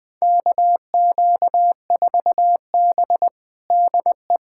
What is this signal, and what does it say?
Morse code, heard as a keyed tone, about 20 wpm, 700 Hz. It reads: KQ4B DE
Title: USA Callsign DE Callsign - 4 Characters - 20wpm